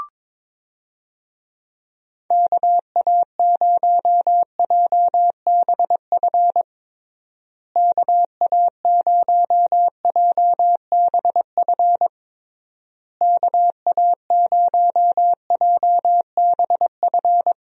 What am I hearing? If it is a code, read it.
Morse code, heard as a keyed tone, about 22 wpm, 700 Hz. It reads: KA0JBF KA0JBF KA0JBF
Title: USA Call Signs - 6 Characters - Repeated 3x - 22wpm